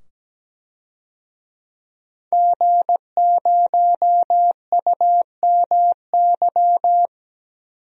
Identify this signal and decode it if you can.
Morse code, heard as a keyed tone, about 17 wpm, 700 Hz. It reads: G0UMY
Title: DXCC-100 Call Signs - 5 Characters - 17wpm